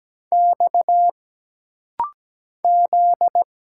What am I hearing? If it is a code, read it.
Morse code, heard as a keyed tone, about 17 wpm, 700 Hz. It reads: X Z
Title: Single Letters - Rapid-Fire 17wpm